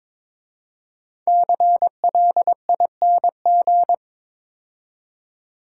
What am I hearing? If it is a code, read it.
Morse code, heard as a keyed tone, about 22 wpm, 690 Hz. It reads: CLING
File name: 5 Letter Words - First 3 Letters Given - 22wpm